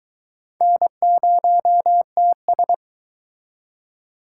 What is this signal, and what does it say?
Morse code, heard as a keyed tone, about 22 wpm, 700 Hz. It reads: N0TS